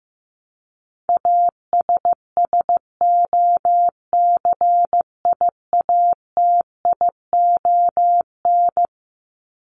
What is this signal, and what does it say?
Morse code, heard as a keyed tone, about 15 wpm, 700 Hz. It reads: ASSOCIATION
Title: Top 1001-1250 Words 15wpm